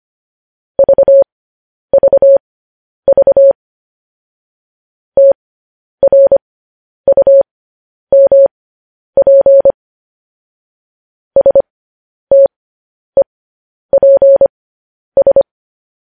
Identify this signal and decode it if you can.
Morse code, heard as a keyed tone, about 25 wpm, spaced as for 10 wpm, 550 Hz. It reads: VVV TRUMP STEPS
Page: news headlines in morse code at 10 wpm